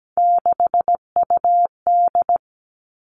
Morse code, heard as a keyed tone, about 17 wpm, 700 Hz. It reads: 6UD
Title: Three Challenging Letter-Number Combinations 17wpm